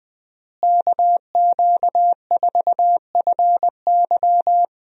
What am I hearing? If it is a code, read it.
Morse code, heard as a keyed tone, about 20 wpm, 700 Hz. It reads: KQ4FY
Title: USA Call Signs - 5 Characters - Repeated 2x - 20wpm